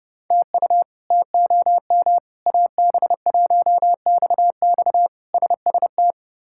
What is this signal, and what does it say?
Morse code, heard as a keyed tone, about 30 wpm, 700 Hz. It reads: TU TOM AB1XX SST
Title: K1USN SST QSO Practice - 30wpm